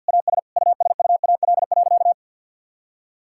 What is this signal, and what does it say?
Morse code, heard as a keyed tone, about 50 wpm, 700 Hz. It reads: AS USUAL.
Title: Sentences from Top 1000 Words - Review Part 6 - 50wpm